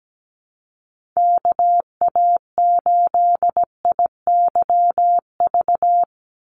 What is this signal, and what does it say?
Morse code, heard as a keyed tone, about 17 wpm, 700 Hz. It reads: KA8IYV
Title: USA Call Signs 17wpm